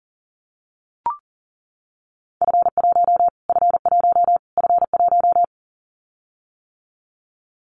Morse code, heard as a keyed tone, about 40 wpm, 700 Hz. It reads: F1 F1 F1
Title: Callsign Prefixes for Top 25 DXCC - Repeated 3x - 40wpm